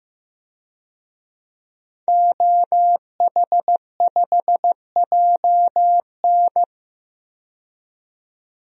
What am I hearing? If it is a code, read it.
Morse code, heard as a keyed tone, about 15 wpm, 700 Hz. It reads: OH5JN
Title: DXCC-100 Call Signs - 5 Characters - 15wpm